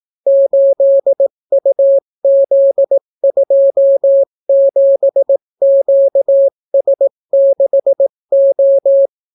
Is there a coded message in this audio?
Yes: Morse code from 0.3 to 9.1 s, heard as a keyed tone, about 18 wpm, 550 Hz. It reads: 8UZ27QS6O